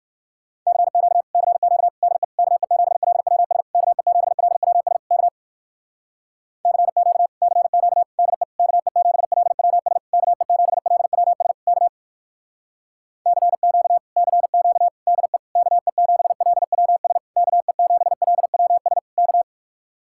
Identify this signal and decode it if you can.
Morse code, heard as a keyed tone, about 58 wpm, 700 Hz. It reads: CQ CQ DE KE7LWS KE7LWS K CQ CQ DE KE7LWS KE7LWS K CQ CQ DE KE7LWS KE7LWS K